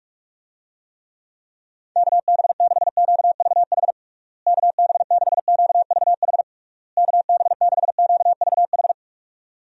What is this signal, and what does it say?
Morse code, heard as a keyed tone, about 41 wpm, 700 Hz. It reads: KB6QVH KB6QVH KB6QVH
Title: USA Call Signs - 6 Characters - Repeated 3x - 45wpm